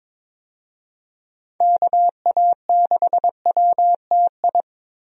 Morse code, heard as a keyed tone, about 22 wpm, 700 Hz. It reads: KA6WTI
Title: USA Call Signs - 6 Characters - 22wpm